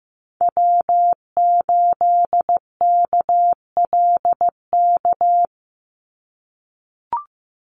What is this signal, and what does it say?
Morse code, heard as a keyed tone, about 15 wpm, 700 Hz. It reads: W8KLK
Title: USA Call Signs 15wpm